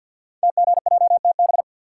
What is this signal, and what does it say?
Morse code, heard as a keyed tone, about 48 wpm, 700 Hz. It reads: TGJTB
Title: Code Groups - 5 Letters - 50wpm